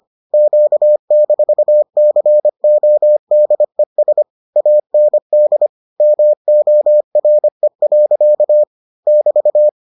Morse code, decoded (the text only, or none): Q-CODES AND MORE. <BT>